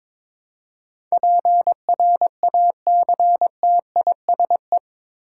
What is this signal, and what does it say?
Morse code, heard as a keyed tone, about 22 wpm, 700 Hz. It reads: PRACTISE
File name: practice / 22wpm / Eight Letter Words - 22wpm